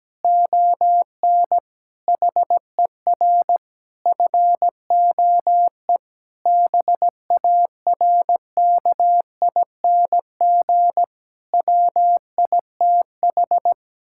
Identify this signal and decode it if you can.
Morse code, heard as a keyed tone, about 17 wpm, 700 Hz. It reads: ON HER FOE BARKING WITH
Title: Pangram Challenge v2 - 17wpm